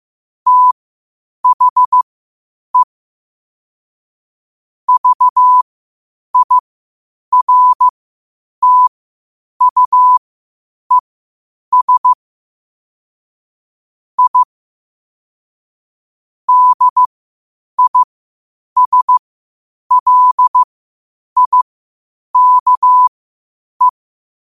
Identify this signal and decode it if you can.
Morse code, heard as a keyed tone, about 15 wpm, spaced as for 8 wpm, 1000 Hz. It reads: THE VIRTUES I DISLIKE
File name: qotdfiles / 8wpm